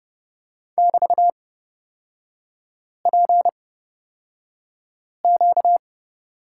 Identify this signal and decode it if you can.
Morse code, heard as a keyed tone, about 31 wpm, spaced as for 5 wpm, 700 Hz. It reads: <BT>PQ
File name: Lesson 38e - Letter q - Three Characters 30wpm